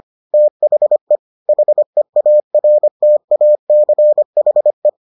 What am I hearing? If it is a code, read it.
Morse code, heard as a keyed tone, about 25 wpm, 600 Hz. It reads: THE HEARTACHE